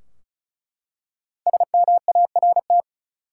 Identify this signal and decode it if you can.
Morse code, heard as a keyed tone, about 36 wpm, 700 Hz. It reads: SMART
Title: Five Letter Words - 35wpm